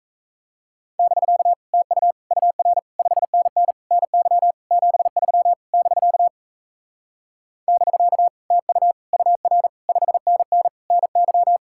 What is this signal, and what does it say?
Morse code, heard as a keyed tone, about 38 wpm, 700 Hz. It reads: <BK> TU UR 5NN NY 73 <BK> <BK> TU UR 5NN NY